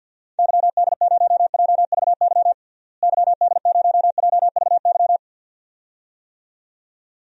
Morse code, heard as a keyed tone, about 51 wpm, 700 Hz. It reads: YD0JVY YD0JVY
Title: DXCC-100 Call Signs - 6 Characters - Repeated 2x - 50wpm